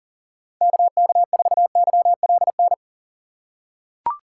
Morse code, heard as a keyed tone, about 37 wpm, 700 Hz. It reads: KK4YLN
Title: USA Call Signs - 6 Characters - 3x Character Spacing - 40wpm